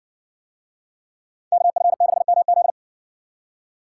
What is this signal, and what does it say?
Morse code, heard as a keyed tone, about 62 wpm, 700 Hz. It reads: KF6MZ